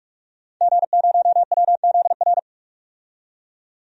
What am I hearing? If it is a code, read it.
Morse code, heard as a keyed tone, about 46 wpm, 700 Hz. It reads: G0WZR
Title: DXCC-100 Call Signs - 5 Characters - 45wpm